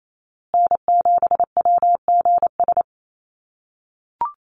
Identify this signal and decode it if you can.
Morse code, heard as a keyed tone, about 28 wpm, 700 Hz. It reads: N7WGS
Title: USA Call Signs Repeated 3x 28wpm